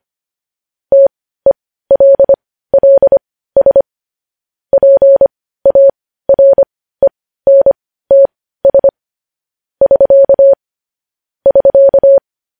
Morse code, heard as a keyed tone, about 25 wpm, spaced as for 15 wpm, 550 Hz. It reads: TELLS PARENTS <SK> <SK>
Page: news-headlines-in-morse-code-at-15-wpm